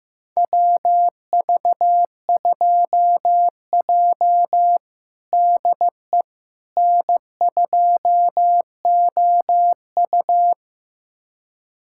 Morse code, heard as a keyed tone, about 15 wpm, 700 Hz. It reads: WV2J DE N2OU
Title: USA Callsign DE Callsign - 4 Characters - 15wpm